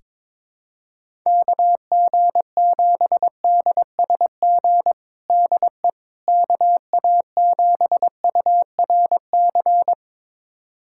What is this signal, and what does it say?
Morse code, heard as a keyed tone, about 22 wpm, 700 Hz. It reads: KG7DSG DE KA7URC
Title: USA Callsign DE Callsign - 6 Characters - 22wpm